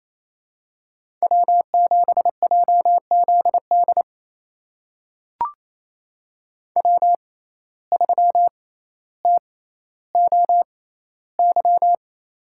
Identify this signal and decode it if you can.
Morse code, heard as a keyed tone, about 28 wpm, spaced as for 14 wpm, 700 Hz. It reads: W7JZD W3TOY
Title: USA Call Signs - 5 Characters - 6x Character Spacing - 28wpm